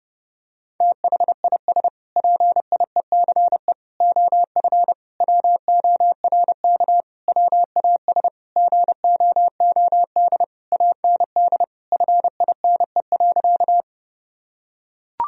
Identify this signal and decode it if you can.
Morse code, heard as a keyed tone, about 31 wpm, 700 Hz. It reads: THIS PIECE OF WORK WAS GOOD AND FINE.
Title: Sentences from Top 1000 Words - Review Part 2 - 30wpm